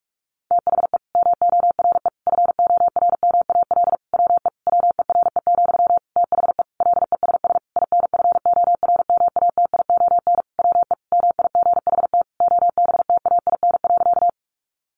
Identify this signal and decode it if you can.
Morse code, heard as a keyed tone, about 44 wpm, 700 Hz. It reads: THE MORE FORMAL WE WERE, THE LESS INFORMATION WE MIGHT OBTAIN.
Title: Sentences from Top 1300 Words 45wpm